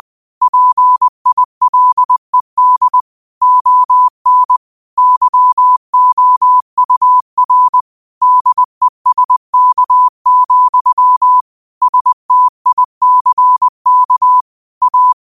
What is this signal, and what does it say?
Morse code, heard as a keyed tone, about 20 wpm, 1000 Hz. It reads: PILED ON YOUR DESK, STICK A